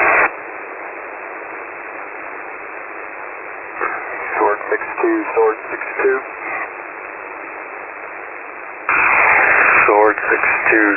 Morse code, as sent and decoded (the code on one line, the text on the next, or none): none